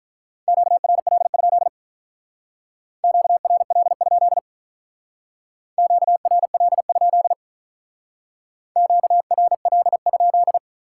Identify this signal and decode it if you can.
Morse code, heard as a keyed tone, about 48 wpm, 700 Hz. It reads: QRL? QRL? QRL? QRL?